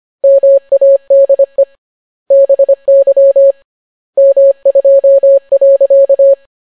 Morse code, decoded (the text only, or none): MADE BY M2.